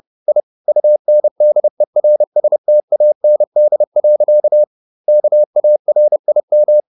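Morse code, heard as a keyed tone, about 30 wpm, 600 Hz. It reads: I UNDERSTAND. KARIM
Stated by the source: Morse code at 30 wpm, the novel Never by Ken Follett in English, from CW med Gnister.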